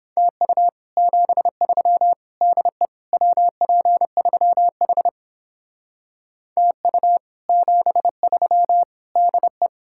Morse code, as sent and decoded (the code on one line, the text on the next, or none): - ..- / --... ...-- / -.. / .-- .--. ...-- .... / - ..- / --... ...-- / -.. .
TU 73 DE WP3H TU 73 DE